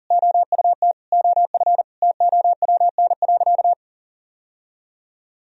Morse code, heard as a keyed tone, about 40 wpm, 700 Hz. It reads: OUT OF TOWN.
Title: Sentences from Top 1000 Words - Review Part 4 - 40wpm